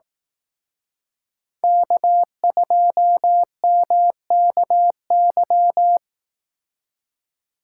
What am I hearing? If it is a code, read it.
Morse code, heard as a keyed tone, about 18 wpm, 700 Hz. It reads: K2MKY
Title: Speed Racing - USA 1x3 Call Signs 15wpm